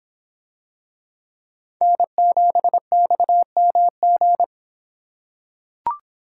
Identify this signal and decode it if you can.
Morse code, heard as a keyed tone, about 24 wpm, 700 Hz. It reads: N7XMG